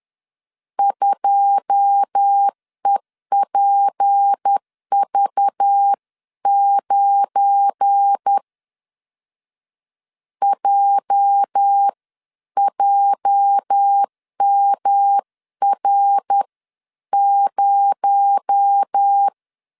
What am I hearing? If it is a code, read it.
Morse code, heard as a keyed tone, about 10 wpm, spaced as for 9 wpm, 800 Hz. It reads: 2EPV9 JJMR0